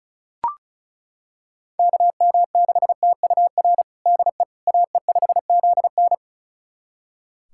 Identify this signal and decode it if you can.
Morse code, heard as a keyed tone, about 34 wpm, 700 Hz. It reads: KM6TUR DE AE5ZN
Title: USA Callsign DE Callsign 35wpm